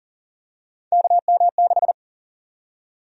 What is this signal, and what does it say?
Morse code, heard as a keyed tone, about 40 wpm, 700 Hz. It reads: KM6
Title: USA Call Sign Prefixes - Repeated 2x - 40wpm